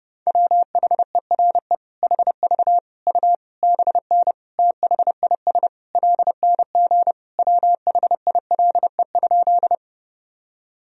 Morse code, heard as a keyed tone, about 32 wpm, 700 Hz. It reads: WHERE HV U BN THIS LNG WHILE?
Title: Sentences from Top 300 Words and Abbreviations 30wpm